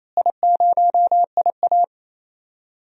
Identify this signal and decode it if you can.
Morse code, heard as a keyed tone, about 28 wpm, 700 Hz. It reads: I0IA
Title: DXCC-100 Call Signs - 4 Characters - Repeated 2x - 28wpm